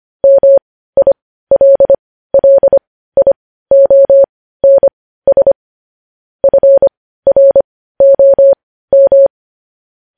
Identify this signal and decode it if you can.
Morse code, heard as a keyed tone, about 25 wpm, spaced as for 15 wpm, 550 Hz. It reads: MILLIONS FROM
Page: News headlines in Morse code at 15 wpm